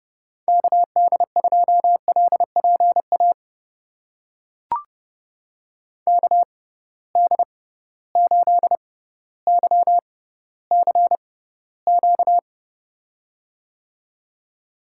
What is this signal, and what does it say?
Morse code, heard as a keyed tone, about 30 wpm, spaced as for 15 wpm, 700 Hz. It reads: KD2LPA KD8YCQ
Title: USA Call Signs - 6 Characters - 6x Character Spacing - 30wpm